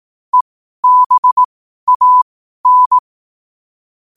Morse code, heard as a keyed tone, about 18 wpm, spaced as for 12 wpm, 1000 Hz. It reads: EBAN
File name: qotdfiles / 12wpm